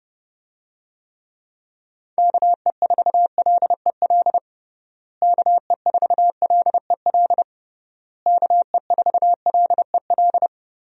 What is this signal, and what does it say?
Morse code, heard as a keyed tone, about 30 wpm, 700 Hz. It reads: KE4LEL KE4LEL KE4LEL
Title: USA Call Signs - 6 Characters - Repeated 3x - 30wpm